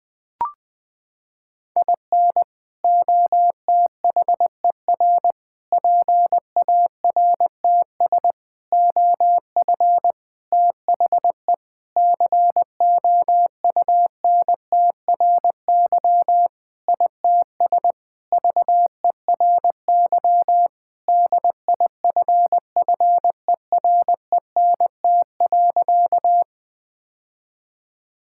Morse code, heard as a keyed tone, about 21 wpm, 700 Hz. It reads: IN OTHER PARTS OF THE COUNTRY ITS VERY DIFFERENT.
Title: Sentences with 5-Word Phrases - 20wpm